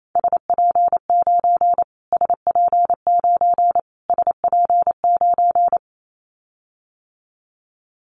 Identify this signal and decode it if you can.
Morse code, heard as a keyed tone, about 28 wpm, 700 Hz. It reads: SP9 SP9 SP9
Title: Callsign Prefixes for Top 25 DXCC - Repeated 3x - 28wpm